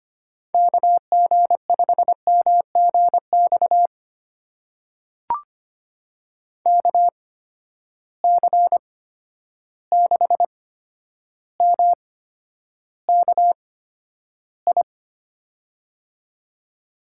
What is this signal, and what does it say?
Morse code, heard as a keyed tone, about 25 wpm, spaced as for 11 wpm, 700 Hz. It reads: KG5MGX KC6MKI
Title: USA Call Signs - 6 Characters - 8x Character Spacing - 25wpm